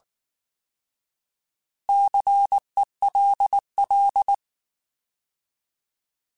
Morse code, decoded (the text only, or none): CELL